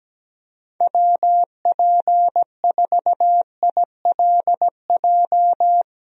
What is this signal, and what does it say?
Morse code, heard as a keyed tone, about 17 wpm, 700 Hz. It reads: WP4ILJ
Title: USA Call Signs - 6 Characters - 17wpm